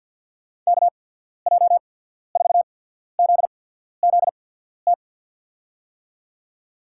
Morse code, heard as a keyed tone, about 53 wpm, 700 Hz. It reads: K J 4 Z Z T
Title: USA Call Signs - 6 Characters - 8x Character Spacing - 50wpm